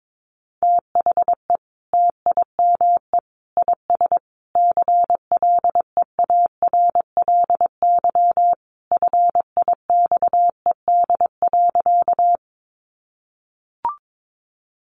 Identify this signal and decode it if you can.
Morse code, heard as a keyed tone, about 22 wpm, 700 Hz. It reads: THE TIME IS CLEARLY FIXED.
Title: Sentences from Top 1300 Words 22wpm